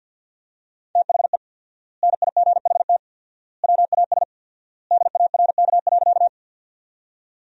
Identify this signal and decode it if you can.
Morse code, heard as a keyed tone, about 53 wpm, 700 Hz. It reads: THE NIGHT WAS DARK.
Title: 4x Word-Spacing - Sentences from Top 500 Words - 50wpm